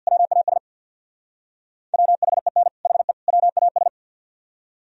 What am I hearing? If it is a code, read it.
Morse code, heard as a keyed tone, about 64 wpm, 700 Hz. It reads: WAS WHEN HE WAS